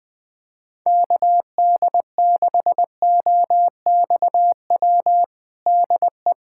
Morse code, heard as a keyed tone, about 20 wpm, 700 Hz. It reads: KD6OXW DE
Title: USA Callsign DE Callsign - 6 Characters - 20wpm